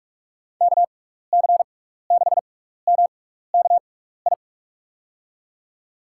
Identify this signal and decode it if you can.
Morse code, heard as a keyed tone, about 45 wpm, spaced as for 15 wpm, 700 Hz. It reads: KC6MKI
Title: USA Call Signs - 6 Characters - 6x Character Spacing - 45wpm